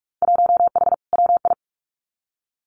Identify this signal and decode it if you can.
Morse code, heard as a keyed tone, about 44 wpm, 700 Hz. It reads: 1H WI
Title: Winter Field Day Exchanges 45wpm